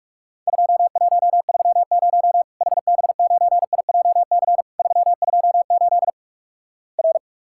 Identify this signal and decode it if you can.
Morse code, heard as a keyed tone, about 46 wpm, 700 Hz. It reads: 2130 HB9IJC 328 R
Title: Worked All Europe DX Contest - QTC Exchange - 45wpm